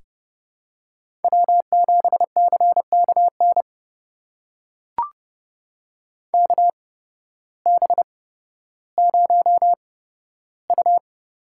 Morse code, heard as a keyed tone, about 30 wpm, spaced as for 14 wpm, 700 Hz. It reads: W7CKN KB0U
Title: USA Call Signs - 5 Characters - 8x Character Spacing - 30wpm